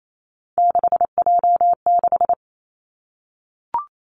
Morse code, heard as a keyed tone, about 28 wpm, 690 Hz. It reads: 6J6
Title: Three Challenging Letter-Number Combinations 28wpm